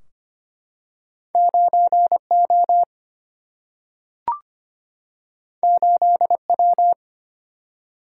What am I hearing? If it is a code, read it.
Morse code, heard as a keyed tone, about 25 wpm, 700 Hz. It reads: 9 O 8 W
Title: Single-Letter Single-Number 25wpm